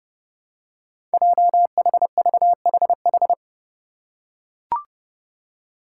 Morse code, heard as a keyed tone, about 30 wpm, 700 Hz. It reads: JHVHH
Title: Five Challenging Letter-Number Combinations 30wpm